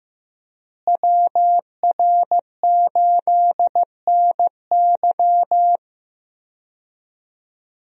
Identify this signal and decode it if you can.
Morse code, heard as a keyed tone, about 15 wpm, 700 Hz. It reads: WR8NY
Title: USA Call Signs - 5 Characters - Repeated 2x - 15wpm